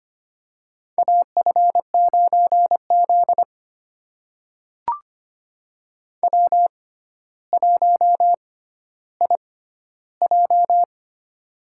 Morse code, heard as a keyed tone, about 25 wpm, spaced as for 13 wpm, 700 Hz. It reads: AF9Z W1IJ
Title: USA Call Signs - 4 Characters - 6x Character Spacing - 25wpm